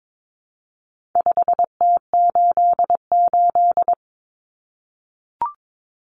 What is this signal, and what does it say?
Morse code, heard as a keyed tone, about 22 wpm, 700 Hz. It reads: 5T88
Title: Common Cut Numbers - 4-Digit 22wpm